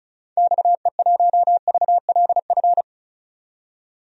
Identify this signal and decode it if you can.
Morse code, heard as a keyed tone, about 34 wpm, 700 Hz. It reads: XE1VLF
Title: DXCC-100 Call Signs - 6 Characters - 35wpm